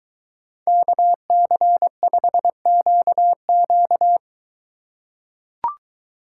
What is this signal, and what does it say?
Morse code, heard as a keyed tone, about 21 wpm, 700 Hz. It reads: KC5QQ